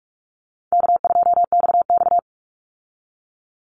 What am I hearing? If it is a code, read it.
Morse code, heard as a keyed tone, about 44 wpm, 700 Hz. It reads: K2XX
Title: Easy Call Signs Repeated 2x - 45wpm